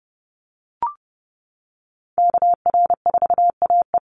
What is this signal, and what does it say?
Morse code, heard as a keyed tone, about 30 wpm, 700 Hz. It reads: KR4AE
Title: Call Signs Repeated 2x - 30wpm